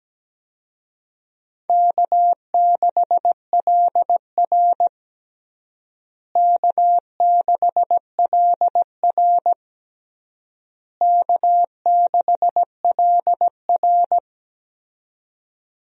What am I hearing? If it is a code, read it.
Morse code, heard as a keyed tone, about 17 wpm, 700 Hz. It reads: K6LR K6LR K6LR
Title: USA Call Signs - 4 Characters - Repeated 3x - 17wpm